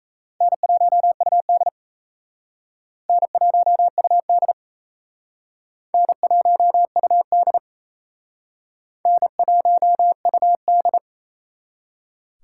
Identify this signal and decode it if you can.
Morse code, heard as a keyed tone, about 35 wpm, 700 Hz. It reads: N1UD N1UD N1UD N1UD